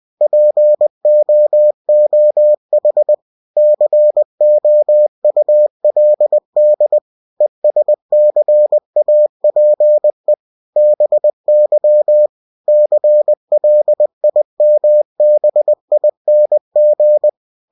Morse code, heard as a keyed tone, about 20 wpm, 600 Hz. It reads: POOH COULD ESCAPE BY CLIMBING
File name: cw-books / winnie-the-pooh / winnie-20wpm-page-0091